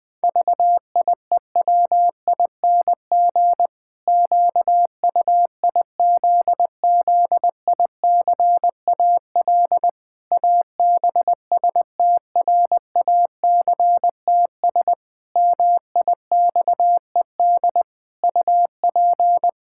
Morse code, heard as a keyed tone, about 20 wpm, 700 Hz. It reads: VIEWING QUIZZICAL ABSTRACTS MIXED UP
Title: Pangram Challenge v2 - 20wpm